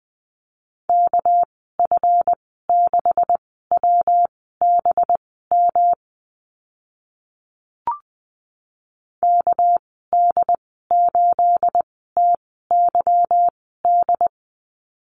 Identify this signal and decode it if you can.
Morse code, heard as a keyed tone, about 20 wpm, spaced as for 14 wpm, 700 Hz. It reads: KF6WBM KD8TYD
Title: 213 Lesson 34g - Number 8 - Call Signs 20wpm